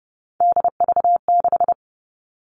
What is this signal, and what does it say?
Morse code, heard as a keyed tone, about 30 wpm, 700 Hz. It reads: DV6
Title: All Cut Numbers - 3-Digit 30wpm